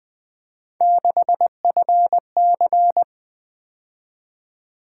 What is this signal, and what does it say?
Morse code, heard as a keyed tone, about 20 wpm, 700 Hz. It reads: 6FC
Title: Three Letter-Number Combinations - 20wpm